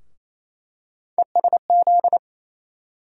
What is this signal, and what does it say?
Morse code, heard as a keyed tone, about 28 wpm, 700 Hz. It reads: ESZ
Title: USA Call Sign Postfixes - 3 Letters 28wpm